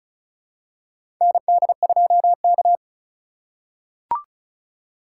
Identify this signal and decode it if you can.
Morse code, heard as a keyed tone, about 36 wpm, 700 Hz. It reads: ND2K
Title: USA Call Signs - 4 Characters - 4x Character Spacing - 35wpm